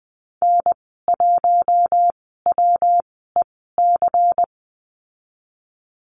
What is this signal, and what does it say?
Morse code, heard as a keyed tone, about 20 wpm, spaced as for 15 wpm, 700 Hz. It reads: N1WEC
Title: 147 Lesson 24g - Letter y - Call Signs 20wpm